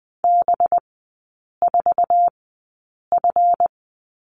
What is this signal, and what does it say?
Morse code, heard as a keyed tone, about 20 wpm, spaced as for 8 wpm, 700 Hz. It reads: B4F
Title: 191 Lesson 31e - Letter v - Three Characters 20wpm